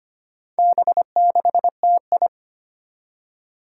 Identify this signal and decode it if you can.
Morse code, heard as a keyed tone, about 25 wpm, 700 Hz. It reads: B6TI
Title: Four Challenging Letter-Number Combinations 25wpm